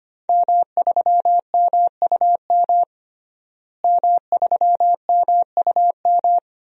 Morse code, heard as a keyed tone, about 25 wpm, 700 Hz. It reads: M3MUM M3MUM
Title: DXCC-100 Call Signs - 5 Characters - Repeated 2x - 25wpm